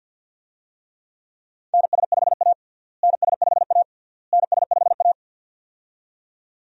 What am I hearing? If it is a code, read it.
Morse code, heard as a keyed tone, about 53 wpm, 700 Hz. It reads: NS5A NS5A NS5A
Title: USA Call Signs - 4 Characters - Repeated 3x - 50wpm